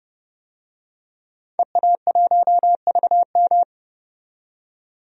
Morse code, heard as a keyed tone, about 32 wpm, 700 Hz. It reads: EA1VM
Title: DXCC-100 Call Signs - 5 Characters - 30wpm